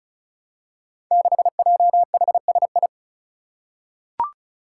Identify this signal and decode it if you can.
Morse code, heard as a keyed tone, about 36 wpm, 700 Hz. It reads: 6JHSI
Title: Five Challenging Letter-Number Combinations 35wpm